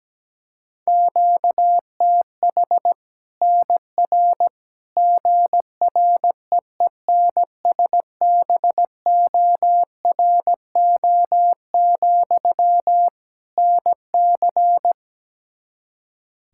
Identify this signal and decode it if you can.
Morse code, heard as a keyed tone, about 17 wpm, 700 Hz. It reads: QTH NR GREENSBORO, NC
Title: Practice QSO - Exchange 1 - Beginning - 17wpm